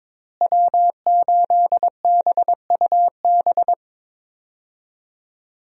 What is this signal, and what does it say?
Morse code, heard as a keyed tone, about 21 wpm, 700 Hz. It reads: W8BUB
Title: USA Call Signs - 5 Characters - Repeated 2x - 22wpm